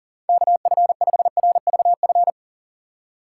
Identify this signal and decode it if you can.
Morse code, heard as a keyed tone, about 40 wpm, 700 Hz. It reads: KF5RVF
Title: USA Call Signs - 4x Character Spacing - 40wpm